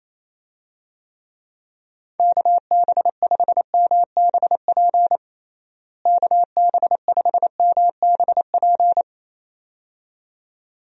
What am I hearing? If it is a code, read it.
Morse code, heard as a keyed tone, about 28 wpm, 700 Hz. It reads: KB5MBP KB5MBP
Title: USA Call Signs - 6 Characters - Repeated 2x - 28wpm